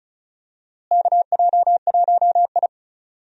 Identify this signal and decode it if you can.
Morse code, heard as a keyed tone, about 36 wpm, 700 Hz. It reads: KJ1I
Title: USA Call Signs - 4 Characters - Repeated 3x - 35wpm